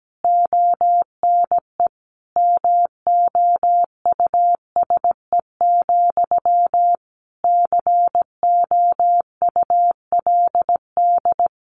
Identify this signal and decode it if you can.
Morse code, heard as a keyed tone, about 17 wpm, 700 Hz. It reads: ONE MOUSE, COULD
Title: A Sound Of Thunder 17wpm